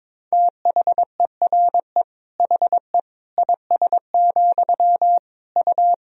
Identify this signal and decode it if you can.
Morse code, heard as a keyed tone, about 22 wpm, 700 Hz. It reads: THERE HE IS, U
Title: Sentences from Top 100 Words and Abbreviations 22wpm